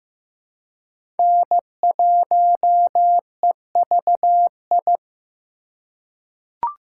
Morse code, heard as a keyed tone, about 15 wpm, 700 Hz. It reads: N1EVI
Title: USA Call Signs - 5 Characters - 4x Character Spacing - 15wpm